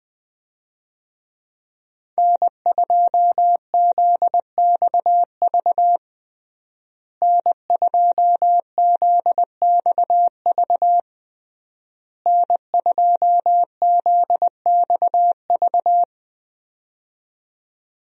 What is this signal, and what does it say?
Morse code, heard as a keyed tone, about 20 wpm, 700 Hz. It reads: N2ZXV N2ZXV N2ZXV
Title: USA Call Signs - 5 Characters - Repeated 3x - 20wpm